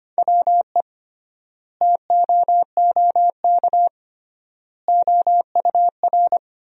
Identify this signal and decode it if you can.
Morse code, heard as a keyed tone, about 25 wpm, 700 Hz. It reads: WE TOOK OUR
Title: 3x Word-Spacing - Sentences from Top 400 Words - 25wpm